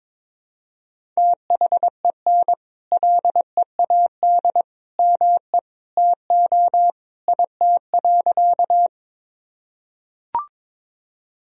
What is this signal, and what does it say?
Morse code, heard as a keyed tone, about 22 wpm, 700 Hz. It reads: THEN LEAD ME TO IT.
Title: Sentences from Top 1000 Words - Review Part 2 - 22wpm